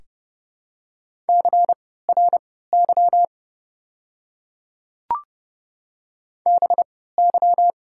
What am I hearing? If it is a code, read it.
Morse code, heard as a keyed tone, about 30 wpm, spaced as for 17 wpm, 700 Hz. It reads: CRY BY
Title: 3x Character Spacing - Top 500 Words - 30wpm